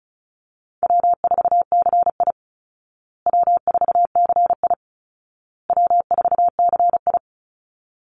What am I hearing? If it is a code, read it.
Morse code, heard as a keyed tone, about 34 wpm, 700 Hz. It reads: W4CI W4CI W4CI
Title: Call Signs Repeated 2x - 35wpm